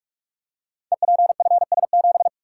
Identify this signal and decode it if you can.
Morse code, heard as a keyed tone, about 42 wpm, 700 Hz. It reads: EPFS7
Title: Code Groups - 5 Letter:Numbers - 45wpm